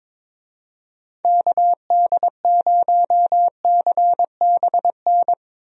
Morse code, heard as a keyed tone, about 22 wpm, 700 Hz. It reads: KD0CBN
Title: USA Call Signs - 6 Characters - Repeated 3x - 22wpm